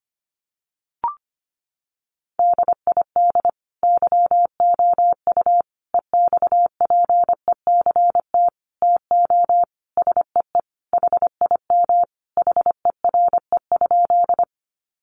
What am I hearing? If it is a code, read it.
Morse code, heard as a keyed tone, about 25 wpm, 700 Hz. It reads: DID YOU EXPECT TO SEE HIM HERE?
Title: Sentences from Top 1100 Words 25wpm